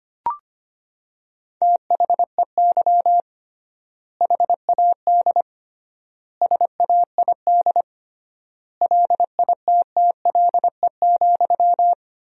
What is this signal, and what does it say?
Morse code, heard as a keyed tone, about 25 wpm, 700 Hz. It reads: THEY HAD SAID LITTLE,
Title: 3x Word-Spacing - Sentences from Top 400 Words - 25wpm